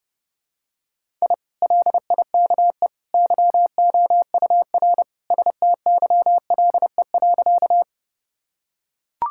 Morse code, heard as a keyed tone, about 30 wpm, 700 Hz. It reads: I LIKE YOUR STYLE.
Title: Sentences from Top 1000 Words - Review Part 5 - 30wpm